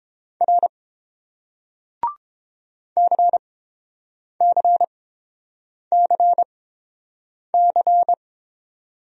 Morse code, heard as a keyed tone, about 30 wpm, spaced as for 7 wpm, 700 Hz. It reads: R CCCC